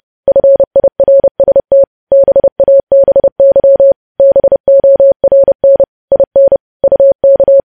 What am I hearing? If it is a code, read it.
Morse code, heard as a keyed tone, about 30 wpm, 550 Hz. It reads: FIRST BABY BORN IN UK
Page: news headlines in morse code at 30 wpm